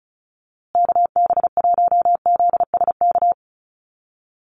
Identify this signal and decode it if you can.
Morse code, heard as a keyed tone, about 34 wpm, 700 Hz. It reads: KB1ZSK
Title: USA Call Signs Repeated 2x 35wpm